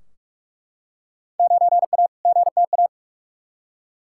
Morse code, heard as a keyed tone, about 41 wpm, 700 Hz. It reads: 9A GTA
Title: Field Day Exchanges Encore - 45wpm